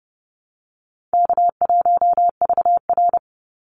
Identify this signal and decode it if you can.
Morse code, heard as a keyed tone, about 30 wpm, 700 Hz. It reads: K1VR
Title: Easy Call Signs - 30wpm